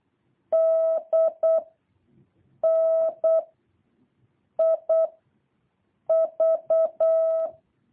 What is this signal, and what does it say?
Morse code, heard as a keyed tone, about 8 wpm, 650 Hz. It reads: D N I V
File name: jh8cbh / 2023cw